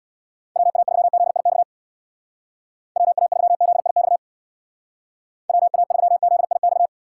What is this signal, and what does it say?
Morse code, heard as a keyed tone, about 59 wpm, 700 Hz. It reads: WA3ZIX WA3ZIX WA3ZIX
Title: Speed Racing - USA Call Signs 50wpm